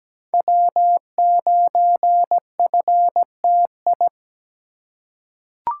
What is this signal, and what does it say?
Morse code, heard as a keyed tone, about 17 wpm, 700 Hz. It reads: W9FTI
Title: USA Call Signs - 2x Character Spacing - 17wpm